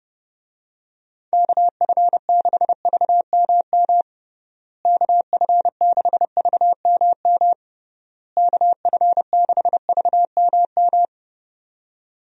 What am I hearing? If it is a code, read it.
Morse code, heard as a keyed tone, about 30 wpm, 700 Hz. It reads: KF6VMM KF6VMM KF6VMM
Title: USA Call Signs - 6 Characters - Repeated 3x - 30wpm